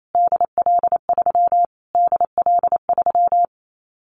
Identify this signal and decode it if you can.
Morse code, heard as a keyed tone, about 28 wpm, 700 Hz. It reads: DL3 DL3
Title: Callsign Prefixes for Top 25 DXCC - Repeated 3x - 28wpm